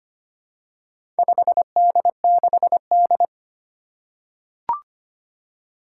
Morse code, heard as a keyed tone, about 25 wpm, 700 Hz. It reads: 5D6D
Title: Four Challenging Letter-Number Combinations 25wpm